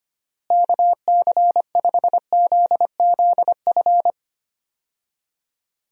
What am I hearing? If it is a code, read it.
Morse code, heard as a keyed tone, about 25 wpm, 700 Hz. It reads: KC5ZZF
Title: USA Call Signs - 6 Characters - Repeated 3x - 25wpm